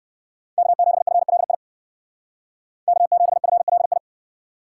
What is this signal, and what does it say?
Morse code, heard as a keyed tone, about 69 wpm, 700 Hz. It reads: K7FLI K7FLI